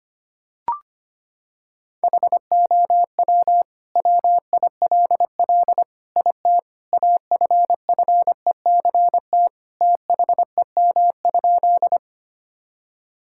Morse code, heard as a keyed tone, about 25 wpm, 700 Hz. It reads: HOW WILL IT AFFECT THEM?
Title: Sentences from Top 1000 Words - Review Part 6 - 25wpm